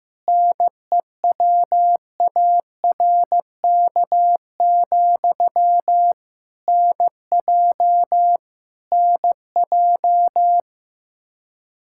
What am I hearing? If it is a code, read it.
Morse code, heard as a keyed tone, about 15 wpm, 700 Hz. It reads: NEWARK, NJ NJ
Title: Practice QSO - Exchange 1 - Beginning - 15wpm